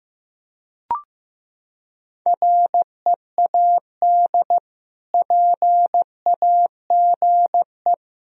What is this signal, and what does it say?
Morse code, heard as a keyed tone, about 16 wpm, 700 Hz. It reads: READ PAGE
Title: Sentences from Top 1000 Words - Review Part 1 - 15wpm